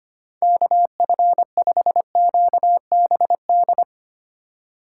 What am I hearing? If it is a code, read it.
Morse code, heard as a keyed tone, about 25 wpm, 700 Hz. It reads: KF5QBD
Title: USA Call Signs - 6 Characters - 3x Character Spacing - 25wpm